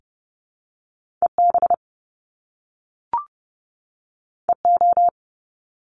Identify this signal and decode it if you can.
Morse code, heard as a keyed tone, about 30 wpm, 700 Hz. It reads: EB EO
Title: Top Two Letter Combinations in Words 30wpm